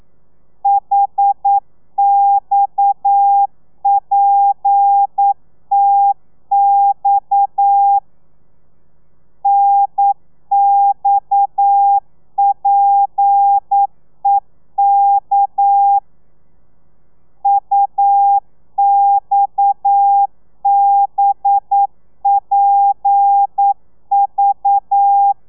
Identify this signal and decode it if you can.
Morse code, heard as a keyed tone, about 9 wpm, 800 Hz. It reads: HXPTX NXPEK UXBPV
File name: telegrafia / i2tfj / Lez7 40